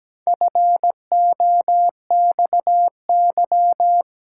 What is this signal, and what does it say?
Morse code, heard as a keyed tone, about 17 wpm, 700 Hz. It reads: FOXY
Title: Pangram Challenge v2 - 17wpm